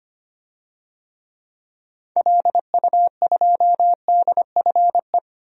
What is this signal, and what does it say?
Morse code, heard as a keyed tone, about 25 wpm, 700 Hz. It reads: LU2DFE
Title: DXCC-100 Call Signs - 6 Characters - 25wpm